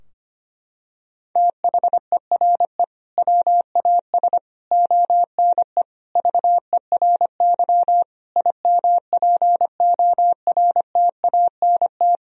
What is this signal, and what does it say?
Morse code, heard as a keyed tone, about 25 wpm, 700 Hz. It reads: THERE WAS ONE VERY IMPORTANT
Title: Sentences from Top 1000 Words - Review Part 3 - 25wpm